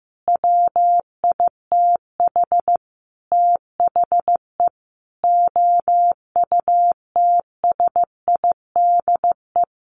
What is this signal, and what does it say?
Morse code, heard as a keyed tone, about 15 wpm, 700 Hz. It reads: WITH THE OUTSIDE
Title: Sentences from Top 1300 Words 15wpm